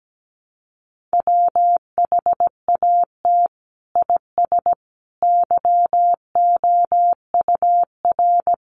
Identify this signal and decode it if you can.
Morse code, heard as a keyed tone, about 17 wpm, 700 Hz. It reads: WHAT IS YOUR
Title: Sentences from Top 600 Words 17wpm